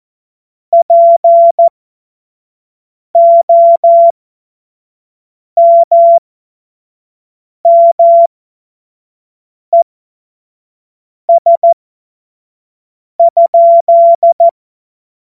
Morse code, heard as a keyed tone, about 14 wpm, spaced as for 5 wpm, 680 Hz. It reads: POMMES?
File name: Morsekurs / Witze / Witz 25. Mai 2021, 25 F70